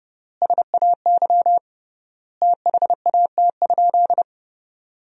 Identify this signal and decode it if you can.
Morse code, heard as a keyed tone, about 31 wpm, 700 Hz. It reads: SAY THAT?
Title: 3x Word-Spacing - Sentences from Top 200 Words - 30wpm